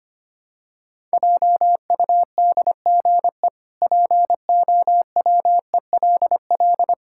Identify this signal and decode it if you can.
Morse code, heard as a keyed tone, about 25 wpm, 700 Hz. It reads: JUDGE POWELL
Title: Pangram Challenge v2 - 25wpm